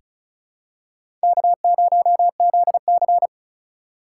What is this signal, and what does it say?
Morse code, heard as a keyed tone, about 36 wpm, 700 Hz. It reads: K0ZC
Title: USA Call Signs - 3x Character Spacing - 35wpm